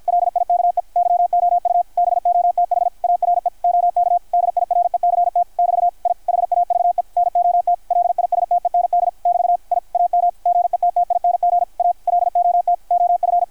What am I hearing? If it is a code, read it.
Morse code, heard as a keyed tone, about 53 wpm, 660 Hz, with radio noise. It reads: LIKE YOU BOTH ARE OK DIRECT <BT> I HAVE NOT LISTEND <BT> I AM GETTING A LOT OF
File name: archived / CRSnet